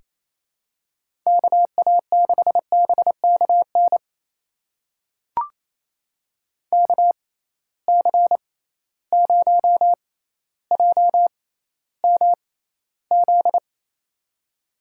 Morse code, heard as a keyed tone, about 28 wpm, spaced as for 14 wpm, 700 Hz. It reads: KA6BKN KC0JMZ
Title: USA Call Signs - 6 Characters - 6x Character Spacing - 28wpm